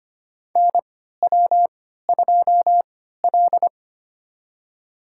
Morse code, heard as a keyed tone, about 25 wpm, spaced as for 14 wpm, 700 Hz. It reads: NW2L
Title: USA Call Signs - 4 Characters - 3x Character Spacing - 25wpm